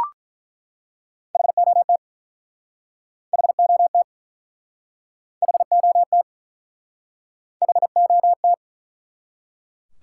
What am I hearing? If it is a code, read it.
Morse code, heard as a keyed tone, about 48 wpm, 700 Hz. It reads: HOT HOT HOT HOT